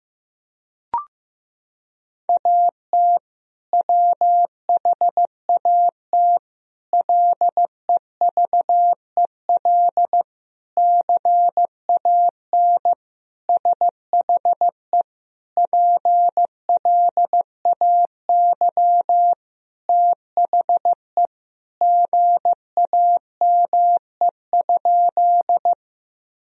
Morse code, heard as a keyed tone, about 15 wpm, 700 Hz. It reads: AT WHAT LEVEL CAN SHE PLAY THE GAME?
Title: Sentences from Top 1000 Words - Review Part 4 - 15wpm